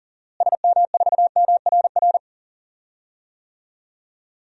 Morse code, heard as a keyed tone, about 40 wpm, 700 Hz. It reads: SM4MRR
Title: DXCC-100 Call Signs - 6 Characters - Repeated 2x - 40wpm